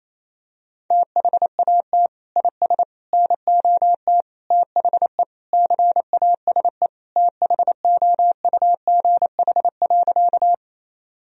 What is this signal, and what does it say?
Morse code, heard as a keyed tone, about 28 wpm, 700 Hz. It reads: THAT IS NOT THE CASE THOUGH.
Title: Sentences from Top 1000 Words - Review Part 2 - 28wpm